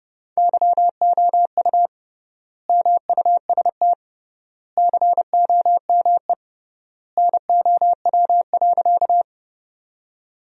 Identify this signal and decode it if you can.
Morse code, heard as a keyed tone, about 30 wpm, 700 Hz. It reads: YOU MUST COME NOW.
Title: 3x Word-Spacing - Sentences from Top 200 Words - 30wpm